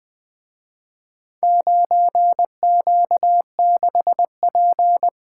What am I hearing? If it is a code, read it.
Morse code, heard as a keyed tone, about 24 wpm, 700 Hz. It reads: 9Q6P
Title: Speed-Racing - Four Letter-Number Combinations - 20wpm